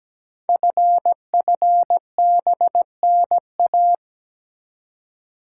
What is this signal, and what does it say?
Morse code, heard as a keyed tone, about 17 wpm, 700 Hz. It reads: FFBNA
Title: Code Groups - 5 Letters - 17wpm